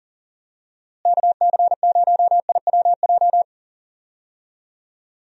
Morse code, heard as a keyed tone, about 40 wpm, 700 Hz. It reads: KC0IWJ
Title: USA Call Signs - 6 Characters - Repeated 3x - 40wpm